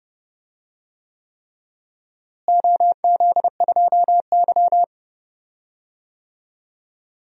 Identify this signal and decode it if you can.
Morse code, heard as a keyed tone, about 30 wpm, 700 Hz. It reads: OZ2Y
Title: DXCC-100 Call Signs - 4 Characters - Repeated 3x - 30wpm